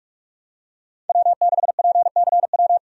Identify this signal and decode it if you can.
Morse code, heard as a keyed tone, about 47 wpm, 700 Hz. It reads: W6PCW
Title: USA Call Signs - 5 Characters - 45wpm